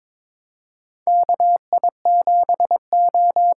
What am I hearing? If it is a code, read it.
Morse code, heard as a keyed tone, about 22 wpm, 700 Hz. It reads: KI7O
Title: USA Call Signs - 6x Character Spacing - 22wpm